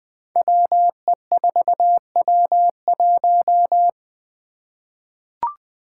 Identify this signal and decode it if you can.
Morse code, heard as a keyed tone, about 20 wpm, 700 Hz. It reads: WE4W1
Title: Five Challenging Letter-Number Combinations 20wpm